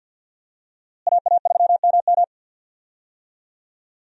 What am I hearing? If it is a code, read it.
Morse code, heard as a keyed tone, about 46 wpm, 700 Hz. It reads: AA3MM
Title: USA Call Signs - 5 Characters - Repeated 3x - 50wpm